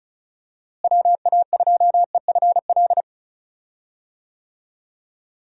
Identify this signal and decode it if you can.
Morse code, heard as a keyed tone, about 34 wpm, 700 Hz. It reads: WA2EFL
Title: USA Call Signs - 6 Characters - Repeated 3x - 35wpm